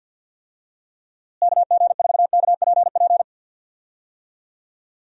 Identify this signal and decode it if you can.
Morse code, heard as a keyed tone, about 45 wpm, 700 Hz. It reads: KG4KPP
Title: USA Call Signs - 6 Characters - Repeated 3x - 50wpm